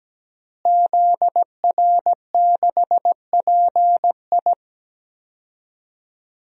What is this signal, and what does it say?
Morse code, heard as a keyed tone, about 17 wpm, 700 Hz. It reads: ZR6PI
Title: DXCC-100 Call Signs - 5 Characters - Repeated 2x - 17wpm